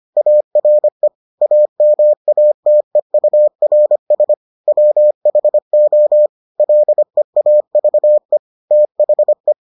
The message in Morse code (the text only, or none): ARE AMATEURS WHO LEAVE THE